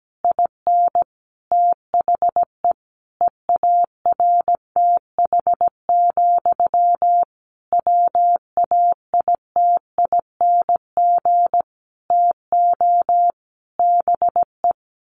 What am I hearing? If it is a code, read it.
Morse code, heard as a keyed tone, about 17 wpm, 700 Hz. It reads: IN THE EARTH, WAITING TO BE
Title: A Sound Of Thunder 17wpm